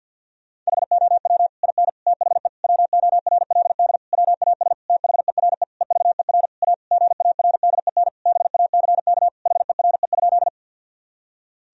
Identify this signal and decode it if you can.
Morse code, heard as a keyed tone, about 51 wpm, 700 Hz. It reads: HOW IN THE WORLD WAS THERE EVER A GARDEN BACK HERE?
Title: Sentences from Top 1000 Words - Review Part 4 - 50wpm